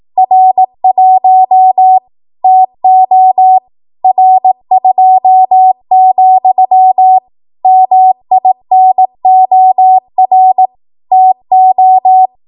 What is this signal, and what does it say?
Morse code, heard as a keyed tone, about 18 wpm, 750 Hz. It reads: R1 TO R2, MINOR TO
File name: Morse / 18WPM